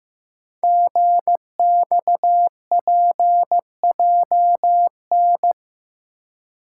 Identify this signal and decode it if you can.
Morse code, heard as a keyed tone, about 15 wpm, 700 Hz. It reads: GXPJN
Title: Code Groups - 5 Letter:Numbers - 15wpm